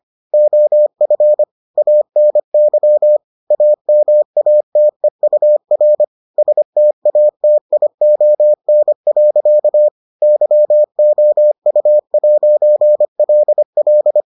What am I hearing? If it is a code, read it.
Morse code, heard as a keyed tone, about 25 wpm, 600 Hz. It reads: OF ANY AMATEUR STATION. YOU'LL